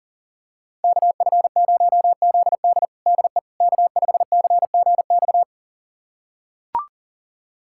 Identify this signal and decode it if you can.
Morse code, heard as a keyed tone, about 37 wpm, 700 Hz. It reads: KF0ZD DE K5CGX
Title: USA Callsign DE Callsign - 5 Characters - 40wpm